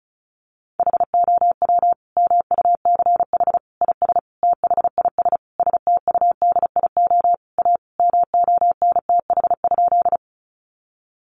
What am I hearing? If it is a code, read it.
Morse code, heard as a keyed tone, about 34 wpm, 700 Hz. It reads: HOW MUCH IS THIS STUDIO A MONTH?
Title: Sentences from Top 1200 Words 35wpm